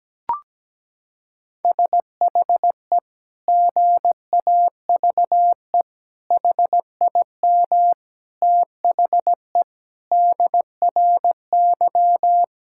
Morse code, heard as a keyed tone, about 17 wpm, 700 Hz. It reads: SHE GAVE HIM THE DRY
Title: Sentences from Top 1600 Words 17wpm